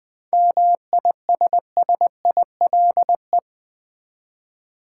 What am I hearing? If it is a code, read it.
Morse code, heard as a keyed tone, about 20 wpm, 700 Hz. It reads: MISSILE
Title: 7 Letter Words - First 3 Letters Given - 20wpm